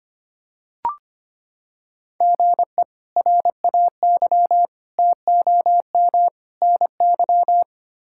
Morse code, heard as a keyed tone, about 25 wpm, 700 Hz. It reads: GE RAY TOM NY
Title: K1USN SST QSO Practice - 25wpm